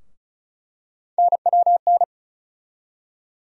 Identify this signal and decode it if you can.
Morse code, heard as a keyed tone, about 36 wpm, 700 Hz. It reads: NWN
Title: USA Call Sign Postfixes - 3 Letters 35wpm